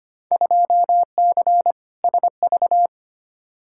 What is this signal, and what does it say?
Morse code, heard as a keyed tone, about 25 wpm, 700 Hz. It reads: 2C SV
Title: Field Day Exchanges Encore - 25wpm